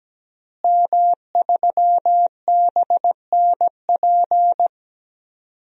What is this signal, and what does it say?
Morse code, heard as a keyed tone, about 17 wpm, 700 Hz. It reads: M3BNP
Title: DXCC-100 Call Signs - 5 Characters - Repeated 3x - 17wpm